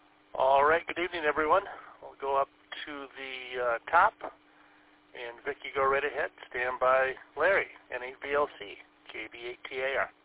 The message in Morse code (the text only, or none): none